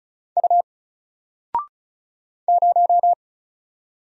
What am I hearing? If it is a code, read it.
Morse code, heard as a keyed tone, about 37 wpm, 700 Hz. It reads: U 0
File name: Single-Letter-Number-35wpm